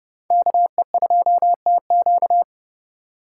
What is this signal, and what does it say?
Morse code, heard as a keyed tone, about 30 wpm, 700 Hz. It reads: KE2TQ